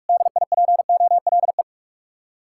Morse code, heard as a keyed tone, about 58 wpm, 700 Hz. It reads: DIPOLE